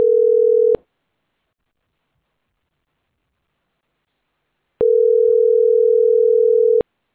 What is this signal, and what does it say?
Ringback tone (North America), 440 + 480 Hz.